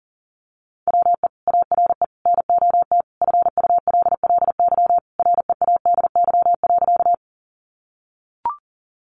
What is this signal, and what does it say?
Morse code, heard as a keyed tone, about 40 wpm, 700 Hz. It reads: WE ARE NOT FULLY READY.
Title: Sentences from Top 1300 Words 40wpm